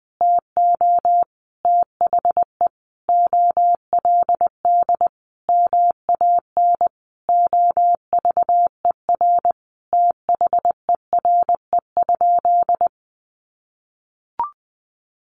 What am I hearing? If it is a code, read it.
Morse code, heard as a keyed tone, about 20 wpm, 700 Hz. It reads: TO THE OLD MAN OVER THERE?
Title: Sentences from Top 200 Words 20wpm